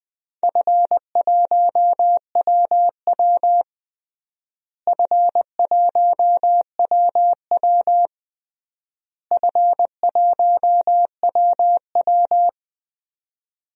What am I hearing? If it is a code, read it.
Morse code, heard as a keyed tone, about 20 wpm, 700 Hz. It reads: F1WW F1WW F1WW
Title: DXCC-100 Call Signs - 4 Characters - Repeated 3x - 20wpm